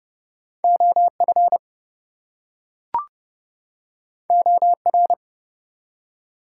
Morse code, heard as a keyed tone, about 31 wpm, 700 Hz. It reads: OF OR
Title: Two Letter Words - 30wpm